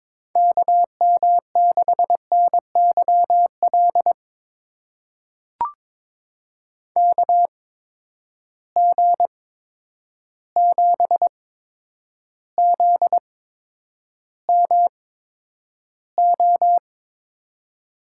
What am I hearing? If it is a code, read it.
Morse code, heard as a keyed tone, about 22 wpm, spaced as for 9 wpm, 700 Hz. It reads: KM6NYL KG7ZMO
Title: USA Call Signs - 6 Characters - 8x Character Spacing - 22wpm